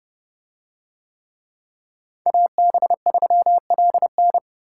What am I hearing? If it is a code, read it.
Morse code, heard as a keyed tone, about 30 wpm, 700 Hz. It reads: AB3LN